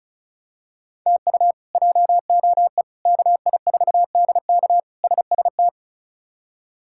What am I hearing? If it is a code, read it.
Morse code, heard as a keyed tone, about 34 wpm, 700 Hz. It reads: TU JOE KI4DK SST
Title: K1USN SST QSO Practice - 35wpm